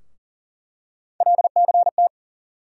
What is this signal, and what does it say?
Morse code, heard as a keyed tone, about 40 wpm, 700 Hz. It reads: LCT
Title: USA Call Sign Postfixes - 3 Letters 40wpm